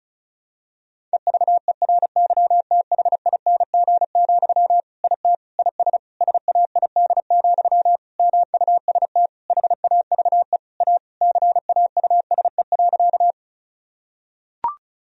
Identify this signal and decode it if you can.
Morse code, heard as a keyed tone, about 36 wpm, 700 Hz. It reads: EVERYTHING, IT IS SAID, MUST HAVE A CAUSE.
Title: Sentences from Top 1000 Words - Review Part 1 - 35wpm